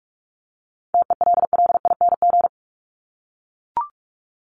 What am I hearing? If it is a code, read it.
Morse code, heard as a keyed tone, about 44 wpm, 700 Hz. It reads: TELLING
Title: ING Suffix - Top 100 Words with Base Words 45wpm